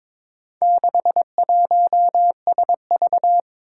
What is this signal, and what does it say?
Morse code, heard as a keyed tone, about 22 wpm, 700 Hz. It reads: 61SV
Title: Four Challenging Letter-Number Combinations 22wpm